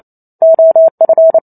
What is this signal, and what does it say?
Morse code, heard as a keyed tone, about 28 wpm, 650 Hz. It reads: OF
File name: Chapter 2 - 28 WPM